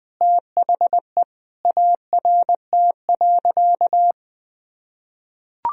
Morse code, heard as a keyed tone, about 20 wpm, 700 Hz. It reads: THE ART.
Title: Sentences from Top 1000 Words - Review Part 3 - 20wpm